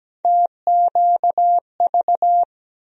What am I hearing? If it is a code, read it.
Morse code, heard as a keyed tone, about 17 wpm, 700 Hz. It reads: TQV
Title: USA Call Sign Postfixes - 3 Letters 17wpm